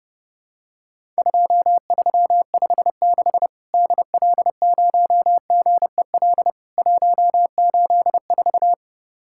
Morse code, heard as a keyed tone, about 32 wpm, 700 Hz. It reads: 2356 DL0GEL 184
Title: Worked All Europe DX Contest - QTC Exchange - 30wpm